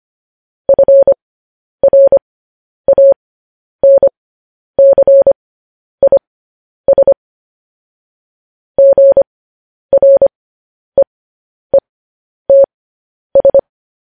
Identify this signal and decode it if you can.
Morse code, heard as a keyed tone, about 25 wpm, spaced as for 10 wpm, 550 Hz. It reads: FRANCIS GREETS
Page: news headlines in morse code at 10 wpm